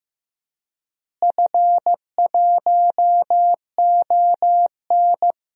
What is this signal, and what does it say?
Morse code, heard as a keyed tone, about 15 wpm, 700 Hz. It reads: F1ON
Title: DXCC-100 Call Signs - 4 Characters - Repeated 3x - 15wpm